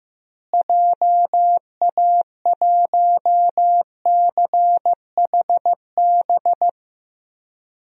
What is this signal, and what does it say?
Morse code, heard as a keyed tone, about 15 wpm, 700 Hz. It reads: JA1CHB
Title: DXCC-100 Call Signs - 6 Characters - Repeated 3x - 15wpm